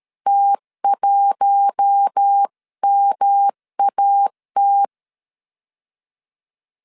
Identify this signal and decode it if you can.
Morse code, heard as a keyed tone, about 13 wpm, 800 Hz. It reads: T1MAT
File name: CW / DET / 1 6 1500C S A 12wpm 3440